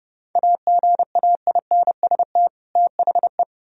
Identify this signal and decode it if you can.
Morse code, heard as a keyed tone, about 37 wpm, 700 Hz. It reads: AGAINST THE
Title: Speed-Racing - Sets of 2 Words 30wpm